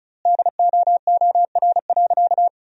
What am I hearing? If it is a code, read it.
Morse code, heard as a keyed tone, about 34 wpm, 700 Hz. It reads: DOOR.
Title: 3x Word-Spacing - Sentences from Top 300 Words - 35wpm